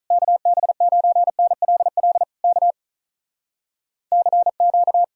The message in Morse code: KB9NLL K CQ

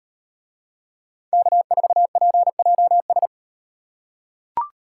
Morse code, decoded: K4PJS

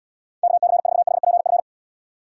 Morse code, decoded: KC4HPF

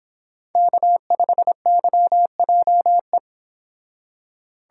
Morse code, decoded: K5YJE

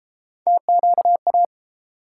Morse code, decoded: TQA

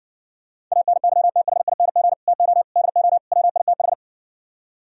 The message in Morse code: ANYTHING TO DO WITH

